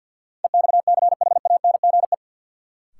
Morse code, heard as a keyed tone, about 53 wpm, 700 Hz. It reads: EXCHANGE
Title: Sentences from Top 1000 Words - Review Part 6 - New Word Drill - 50wpm